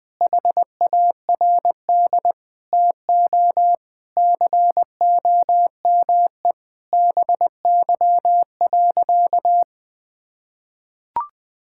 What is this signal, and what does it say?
Morse code, heard as a keyed tone, about 20 wpm, 700 Hz. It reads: HARD TO COME BY.
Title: Sentences from Top 1000 Words - Review Part 5 - 20wpm